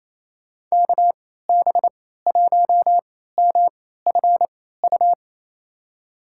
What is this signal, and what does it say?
Morse code, heard as a keyed tone, about 28 wpm, spaced as for 16 wpm, 700 Hz. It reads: KB1MFU
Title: USA Call Signs - 6 Characters - 3x Character Spacing - 28wpm